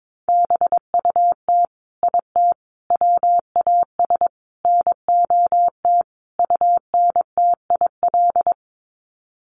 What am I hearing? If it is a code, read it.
Morse code, heard as a keyed tone, about 22 wpm, 700 Hz. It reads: BUT IT WAS NOT UNTIL
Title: Sets of 5 Words Encore2 22wpm